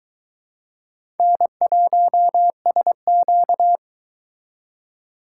Morse code, heard as a keyed tone, about 22 wpm, 700 Hz. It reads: N1SQ